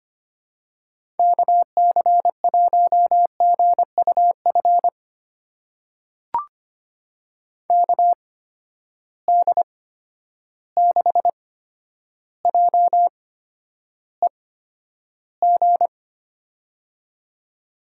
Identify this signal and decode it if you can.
Morse code, heard as a keyed tone, about 25 wpm, spaced as for 11 wpm, 700 Hz. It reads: KC1GUF KD6JEG
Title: USA Call Signs - 8x Character Spacing - 25wpm